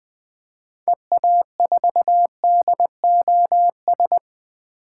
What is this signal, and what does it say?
Morse code, heard as a keyed tone, about 20 wpm, 700 Hz. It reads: EA4DOS
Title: 2023 SOTA Activator Call Signs - 20wpm